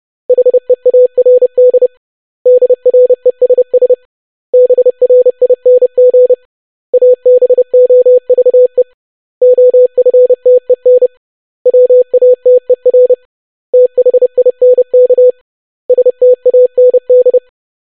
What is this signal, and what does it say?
Morse code, heard as a keyed tone, about 30 wpm, 500 Hz. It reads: HEARD DRESS BRING ABOVE OFTEN WATER THINK STAND
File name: basic500-25wpm